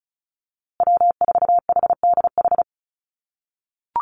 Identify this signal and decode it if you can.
Morse code, heard as a keyed tone, about 34 wpm, 700 Hz. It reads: W4HDH
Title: USA Call Signs Repeated 2x 35wpm